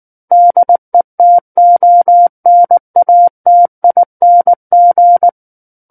Morse code, heard as a keyed tone, about 19 wpm, 700 Hz. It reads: DETONATING
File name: Book One / 10-Chapter 10 - 19 WPM